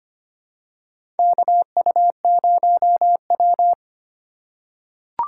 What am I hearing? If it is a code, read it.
Morse code, heard as a keyed tone, about 25 wpm, 700 Hz. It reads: KU0W
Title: USA Call Signs - 4 Characters - 2x Character Spacing - 25wpm